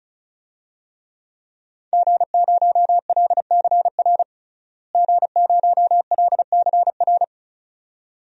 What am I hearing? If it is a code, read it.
Morse code, heard as a keyed tone, about 34 wpm, 700 Hz. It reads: G0LCR G0LCR
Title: DXCC-100 Call Signs - 5 Characters - Repeated 2x - 35wpm